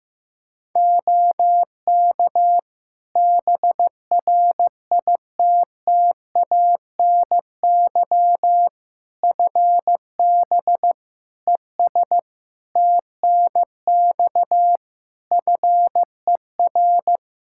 Morse code, heard as a keyed tone, about 15 wpm, 700 Hz. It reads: OK BRITTANY FB ES TNX FER
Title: Practice QSO - Exchange 2 - Info - 15wpm